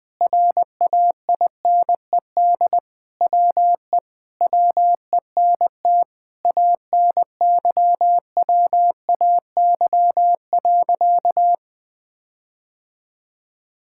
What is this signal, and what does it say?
Morse code, heard as a keyed tone, about 20 wpm, 700 Hz. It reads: RAINED WE WENT ANYWAY.
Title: Sentences with 4-Word Phrases - 20wpm